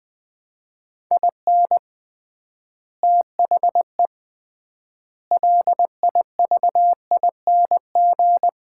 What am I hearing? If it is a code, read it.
Morse code, heard as a keyed tone, about 21 wpm, 700 Hz. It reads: IN THE LIVING
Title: 3x Word-Spacing - Sets of 4 Words - 20wpm